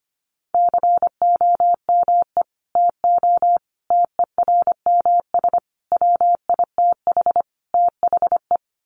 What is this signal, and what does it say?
Morse code, heard as a keyed tone, about 25 wpm, 700 Hz. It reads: COME TO TERMS WITH THE
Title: Sets of 5 Words Encore2 25wpm